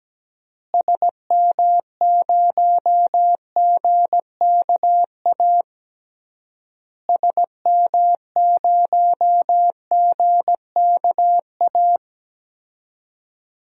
Morse code, heard as a keyed tone, about 17 wpm, 700 Hz. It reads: SM0GKA SM0GKA
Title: DXCC-100 Call Signs - 6 Characters - Repeated 3x - 17wpm